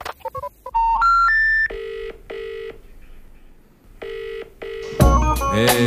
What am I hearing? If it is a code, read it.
Special information tone.